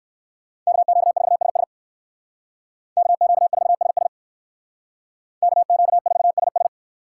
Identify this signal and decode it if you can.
Morse code, heard as a keyed tone, about 56 wpm, 700 Hz. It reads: KQ4SS KQ4SS KQ4SS
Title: Speed Racing - USA 2x2 Call Signs 45wpm